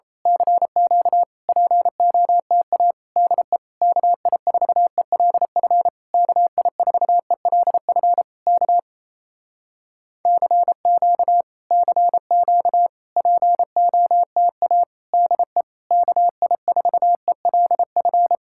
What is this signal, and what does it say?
Morse code, heard as a keyed tone, about 31 wpm, 700 Hz. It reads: CQ POTA DE KI4ELF KI4ELF K CQ CQ POTA DE KI4ELF
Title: Speed-Racing Common QSO Phrases Encore - 28wpm